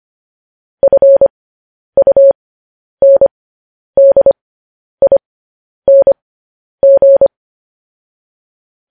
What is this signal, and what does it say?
Morse code, heard as a keyed tone, about 25 wpm, spaced as for 10 wpm, 550 Hz. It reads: FUNDING
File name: podcast / News headlines-1x-25x10